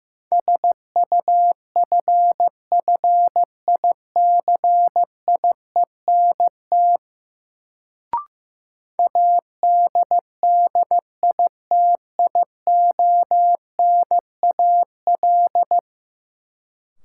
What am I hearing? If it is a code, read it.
Morse code, heard as a keyed tone, about 16 wpm, 700 Hz. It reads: SUFFICIENT ADDITIONAL
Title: Sentences from Top 1400 Words - New Word Drill Alt - 15wpm